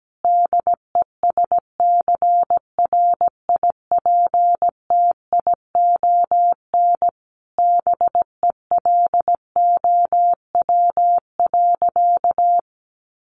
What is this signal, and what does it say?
Morse code, heard as a keyed tone, about 17 wpm, 700 Hz. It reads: DESCRIPTION BELOW.
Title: Sentences from Top 1500 Words 17wpm